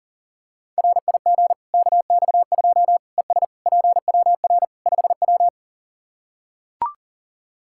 Morse code, heard as a keyed tone, about 40 wpm, 700 Hz. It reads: RIG KX2 ES PWR 5W
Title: Practice QSO - Exchange 2 - Info - 40wpm